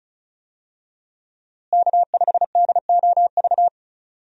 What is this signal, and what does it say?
Morse code, heard as a keyed tone, about 36 wpm, 700 Hz. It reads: K5DOV
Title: USA Call Signs - 5 Characters - Repeated 3x - 35wpm